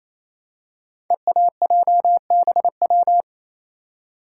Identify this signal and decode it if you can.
Morse code, heard as a keyed tone, about 27 wpm, 700 Hz. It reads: EAJBW
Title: Code Groups - 5 Letters - 28wpm